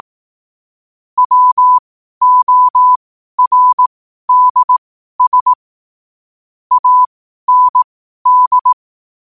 Morse code, heard as a keyed tone, about 18 wpm, spaced as for 12 wpm, 1000 Hz. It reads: WORDS AND
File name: qotdfiles / 12wpm